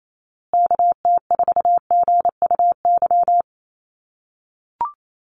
Morse code, heard as a keyed tone, about 28 wpm, 700 Hz. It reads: KT4GUY
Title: USA Call Signs Repeated 2x 28wpm